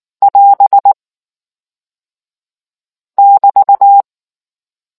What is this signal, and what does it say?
Morse code, heard as a keyed tone, about 19 wpm, 800 Hz. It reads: <AS> <BT>